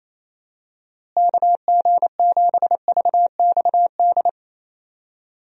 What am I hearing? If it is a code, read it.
Morse code, heard as a keyed tone, about 28 wpm, 700 Hz. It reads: KG7VXD
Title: USA Call Signs - 6 Characters - Repeated 3x - 28wpm